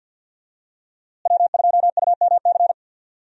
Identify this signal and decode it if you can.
Morse code, heard as a keyed tone, about 52 wpm, 700 Hz. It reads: W2UMC
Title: USA Call Signs - 5 Characters - Repeated 3x - 50wpm